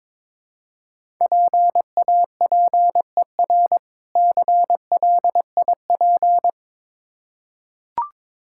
Morse code, heard as a keyed tone, about 22 wpm, 700 Hz. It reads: PAPER CLIP
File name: Difficult Words with Hints - 22wpm